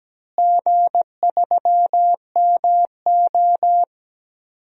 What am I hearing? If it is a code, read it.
Morse code, heard as a keyed tone, about 17 wpm, 700 Hz. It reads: G3MO
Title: DXCC-100 Call Signs - 4 Characters - Repeated 3x - 17wpm